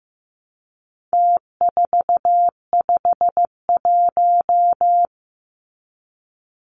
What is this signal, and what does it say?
Morse code, heard as a keyed tone, about 15 wpm, 700 Hz. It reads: T451
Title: Numbers - Contest Style - 4-Digit 15wpm